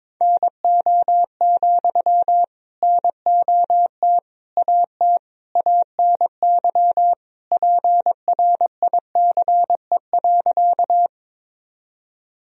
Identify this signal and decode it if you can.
Morse code, heard as a keyed tone, about 22 wpm, 700 Hz. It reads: NO, NOT AT ANY PRICE.
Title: Sentences from Top 1000 Words - Review Part 4 - 22wpm